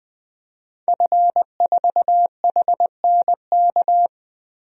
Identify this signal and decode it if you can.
Morse code, heard as a keyed tone, about 21 wpm, 700 Hz. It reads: F4HNK
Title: DXCC-100 Call Signs - 5 Characters - Repeated 2x - 20wpm